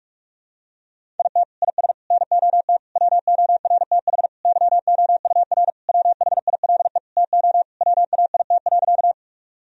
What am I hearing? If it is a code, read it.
Morse code, heard as a keyed tone, about 41 wpm, 700 Hz. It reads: IT IS NOT WORTH YOUR WHILE TO WAIT.
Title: Sentences from Top 1000 Words - Review Part 1 - 45wpm